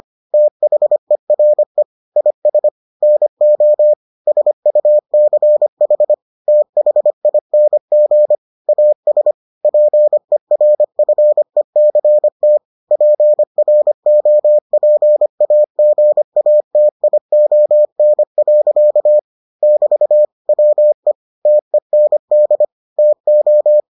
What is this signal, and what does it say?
Morse code, heard as a keyed tone, about 25 wpm, 600 Hz. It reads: THERE IS NO SUCH THING AS PERFECT PROPAGATION. <BT> WE TEND TO